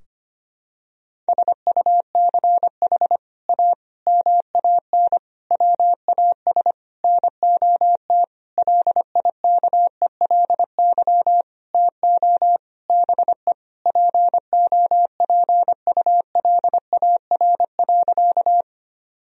Morse code, heard as a keyed tone, about 25 wpm, 700 Hz. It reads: SUCH A MAN WAS NOT LIKELY TO BE POPULAR.
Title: Sentences from Top 1000 Words - Review Part 4 - 25wpm